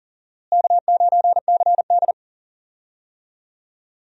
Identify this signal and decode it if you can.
Morse code, heard as a keyed tone, about 40 wpm, 700 Hz. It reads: K9CD
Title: USA Call Signs - 4 Characters - Repeated 2x - 40wpm